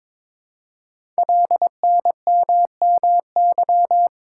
Morse code, heard as a keyed tone, about 22 wpm, 700 Hz. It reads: LNMMY
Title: Code Groups - 5 Letters - 22wpm